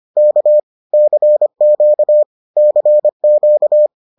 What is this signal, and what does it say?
Morse code, heard as a keyed tone, about 25 wpm, 600 Hz. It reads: K CQ CQ